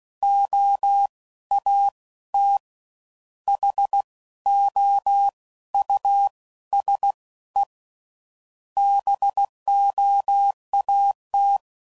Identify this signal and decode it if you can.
Morse code, heard as a keyed tone, about 16 wpm, 800 Hz. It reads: O A T H O U S E BOAT
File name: Suffixes / General / Houses